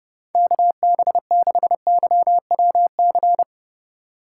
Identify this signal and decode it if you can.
Morse code, heard as a keyed tone, about 37 wpm, 700 Hz. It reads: KB6YWC
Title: Speed Racing - USA 2x3 Call Signs 30wpm